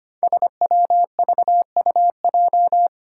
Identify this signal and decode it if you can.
Morse code, heard as a keyed tone, about 25 wpm, 700 Hz. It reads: SWVUJ
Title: Five Challenging Letter-Number Combinations 25wpm